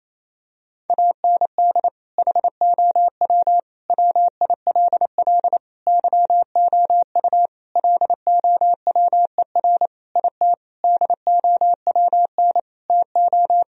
Morse code, heard as a keyed tone, about 28 wpm, 700 Hz. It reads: AND HOW WILL YOU LOWER IT DOWN TO
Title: Sentences from Top 1000 Words - Review Part 4 - 28wpm